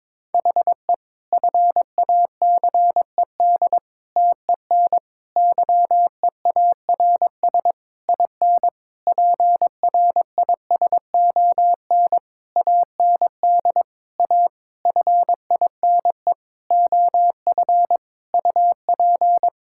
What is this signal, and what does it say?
Morse code, heard as a keyed tone, about 22 wpm, 700 Hz. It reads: HE FACED TEN YEARS IN PRISON AND A FINE OF UP
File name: Sentences with 9-Word Phrases - 22wpm